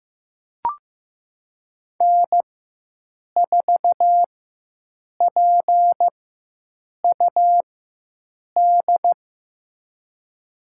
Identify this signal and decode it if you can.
Morse code, heard as a keyed tone, about 15 wpm, spaced as for 7 wpm, 700 Hz. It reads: N4PUD
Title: USA Call Signs - 5 Characters - 4x Character Spacing - 15wpm